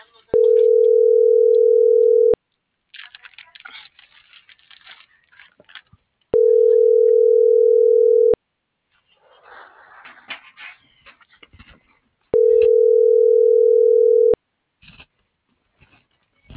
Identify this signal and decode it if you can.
Ringback tone (North America), 440 + 480 Hz.